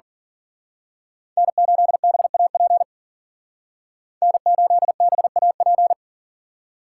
Morse code, heard as a keyed tone, about 41 wpm, 700 Hz. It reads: N8BAP N8BAP
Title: Speed Racing - USA 1x3 Call Signs 40wpm